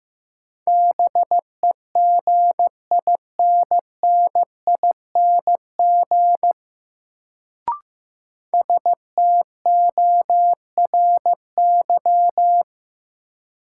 Morse code, heard as a keyed tone, about 15 wpm, 700 Hz. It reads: BEGINNING STORY
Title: Sets of 3 Words - New Word Drill - 15wpm